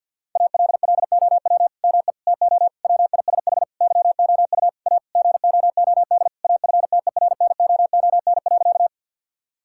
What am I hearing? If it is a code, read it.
Morse code, heard as a keyed tone, about 50 wpm, 700 Hz. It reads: ALLOW ME TO WISH YOU A GOOD AFTERNOON.
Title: Sentences from Top 1600 Words 50wpm